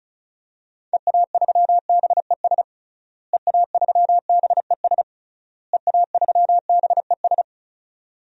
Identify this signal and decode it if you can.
Morse code, heard as a keyed tone, about 34 wpm, 700 Hz. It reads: EA3BES EA3BES EA3BES
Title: DXCC-100 Call Signs - 6 Characters - Repeated 3x - 35wpm